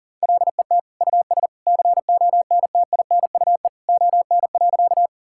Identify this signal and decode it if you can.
Morse code, heard as a keyed tone, about 40 wpm, 700 Hz. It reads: LET US CONTINUE ON.
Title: Sentences from Top 1000 Words - Review Part 6 - 40wpm